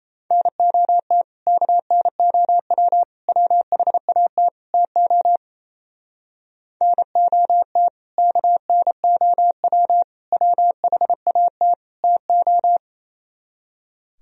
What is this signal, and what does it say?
Morse code, heard as a keyed tone, about 34 wpm, 700 Hz. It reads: NOT KNOW WHAT TO NOT KNOW WHAT TO
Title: Speed-Racing - Sets of 4 Words 28wpm